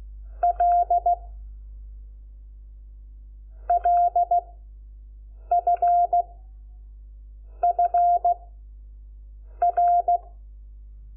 Morse code, heard as a keyed tone, about 16 wpm, 700 Hz. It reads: L L F F R
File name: Lecciones / CLASE Nº5